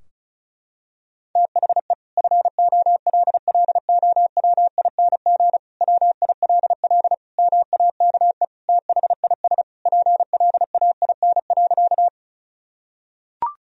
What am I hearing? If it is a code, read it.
Morse code, heard as a keyed tone, about 36 wpm, 700 Hz. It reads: THE FOLLOWING WILL MAKE THIS PLAIN.
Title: Sentences from Top 1000 Words - Review Part 4 - 35wpm